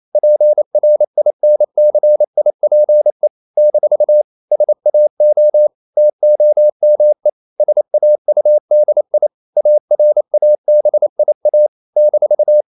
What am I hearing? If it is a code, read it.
Morse code, heard as a keyed tone, about 28 wpm, 600 Hz. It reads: PRINCIPE - SAO TOME SAUDI ARABIA -
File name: CW / Practice / NationalCapitals28